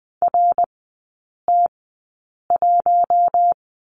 Morse code, heard as a keyed tone, about 20 wpm, spaced as for 8 wpm, 690 Hz. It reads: RT1
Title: 159 Lesson 26e - Letter g - Three Characters 20wpm